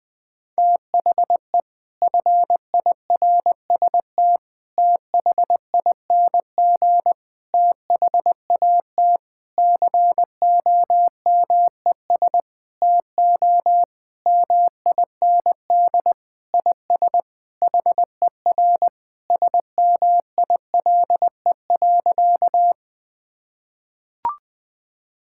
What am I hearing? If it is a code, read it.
Morse code, heard as a keyed tone, about 20 wpm, 700 Hz. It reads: THE FIRST THING THAT COMES TO MIND IS HER SMILE.
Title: Sentences with 8-Word Phrases - Spoken Beginning - 20wpm